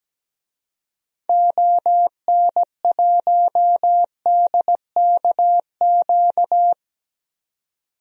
Morse code, heard as a keyed tone, about 17 wpm, 700 Hz. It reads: ON1DKQ
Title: DXCC-100 Call Signs - 6 Characters - 17wpm